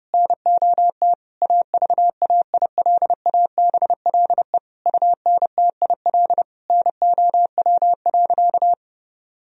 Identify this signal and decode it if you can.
Morse code, heard as a keyed tone, about 30 wpm, 700 Hz. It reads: NOT AVAILABLE UNTIL NOW.
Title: Sentences from Top 1000 Words - Review Part 3 - 30wpm